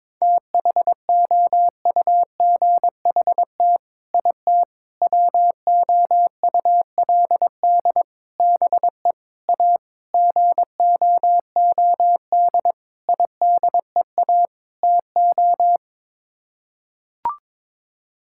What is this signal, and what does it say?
Morse code, heard as a keyed tone, about 22 wpm, 700 Hz. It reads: THOUGHT IT WOULD BE A GOOD IDEA TO
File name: practice / 22wpm / Sets of 8 Words - 22wpm